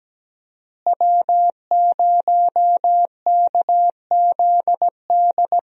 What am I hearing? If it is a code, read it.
Morse code, heard as a keyed tone, about 17 wpm, 700 Hz. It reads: W0KZD
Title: USA Call Signs - 5 Characters - Repeated 2x - 17wpm